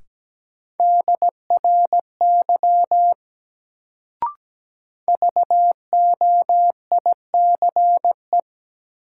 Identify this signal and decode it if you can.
Morse code, heard as a keyed tone, about 17 wpm, 700 Hz. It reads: DRY VOICE
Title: Sentences from Top 500 Words - New Word Drill - 17wpm